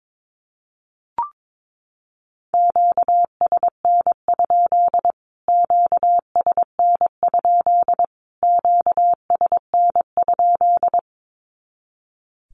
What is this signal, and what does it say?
Morse code, heard as a keyed tone, about 22 wpm, 700 Hz. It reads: QSN? QSN? QSN?
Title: Q-Codes Repeated 3x 22wpm